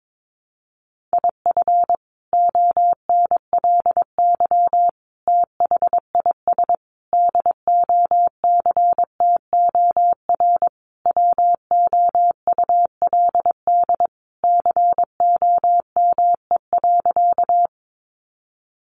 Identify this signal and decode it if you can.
Morse code, heard as a keyed tone, about 22 wpm, 700 Hz. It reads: IF ONLY THIS DOCTOR WOULD COME.
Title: Sentences from Top 1500 Words 22wpm